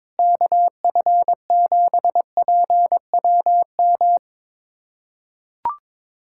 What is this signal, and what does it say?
Morse code, heard as a keyed tone, about 22 wpm, 700 Hz. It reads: KF7PWM
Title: USA Callsign DE Callsign 22wpm